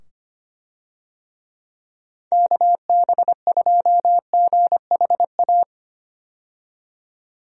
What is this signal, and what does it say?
Morse code, heard as a keyed tone, about 25 wpm, 700 Hz. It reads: KB2GHA
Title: USA Call Signs - 6 Characters - 25wpm